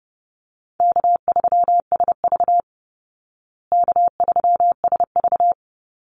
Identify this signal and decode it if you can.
Morse code, heard as a keyed tone, about 30 wpm, 700 Hz. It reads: K3SV K3SV
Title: Easy Call Signs Repeated 2x - 30wpm